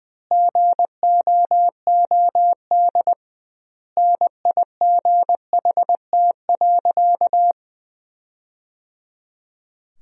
Morse code, heard as a keyed tone, about 21 wpm, 700 Hz. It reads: GOOD NIGHT.
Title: 2x Word-Spacing - Sentences from Top 300 Words - 20wpm